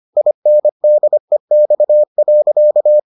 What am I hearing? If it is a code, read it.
Morse code, heard as a keyed tone, about 25 wpm, 600 Hz. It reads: INDEX.